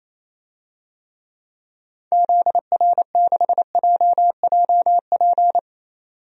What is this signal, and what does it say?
Morse code, heard as a keyed tone, about 28 wpm, 700 Hz. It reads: ZR6JJP
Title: DXCC-100 Call Signs - 6 Characters - 28wpm